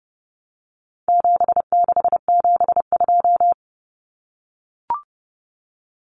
Morse code, heard as a keyed tone, about 30 wpm, 700 Hz. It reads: 7672
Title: Common Cut Numbers - 4-Digit 30wpm